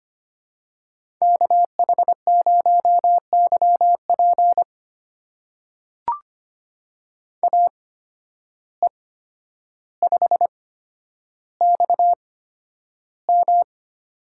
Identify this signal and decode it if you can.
Morse code, heard as a keyed tone, about 25 wpm, spaced as for 11 wpm, 700 Hz. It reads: KH0YP AE5XM
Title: USA Call Signs - 5 Characters - 8x Character Spacing - 25wpm